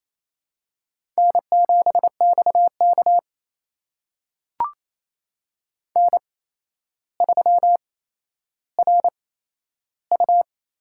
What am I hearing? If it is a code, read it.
Morse code, heard as a keyed tone, about 28 wpm, spaced as for 12 wpm, 700 Hz. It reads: N7XK N3RU
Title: USA Call Signs - 4 Characters - 8x Character Spacing - 28wpm